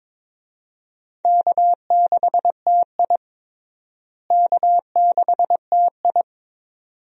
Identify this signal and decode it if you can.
Morse code, heard as a keyed tone, about 22 wpm, 700 Hz. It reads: K6TI K6TI
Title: USA Call Signs - 4 Characters - Repeated 2x - 22wpm